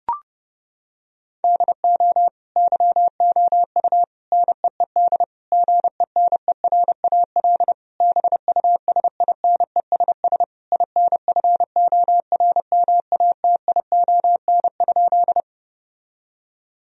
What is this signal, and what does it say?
Morse code, heard as a keyed tone, about 30 wpm, 700 Hz. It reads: DO YOU NEED GENERAL BUSINESS INFORMATION?
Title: Sentences from Top 1000 Words - Review Part 3 - 30wpm